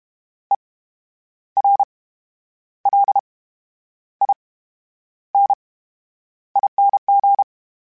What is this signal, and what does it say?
Morse code, heard as a keyed tone, about 32 wpm, 800 Hz. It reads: E R L I N I N G
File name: Special INGs / Special INGs-mp3s / VerySlowToFaster Ing @30